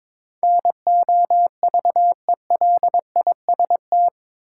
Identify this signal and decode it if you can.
Morse code, heard as a keyed tone, about 22 wpm, 700 Hz. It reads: NOVELIST
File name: practice / 22wpm / Eight Letter Words - 22wpm